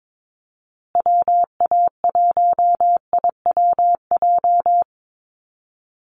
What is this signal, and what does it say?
Morse code, heard as a keyed tone, about 22 wpm, 700 Hz. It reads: WA1IWJ
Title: USA Call Signs Repeated 2x 22wpm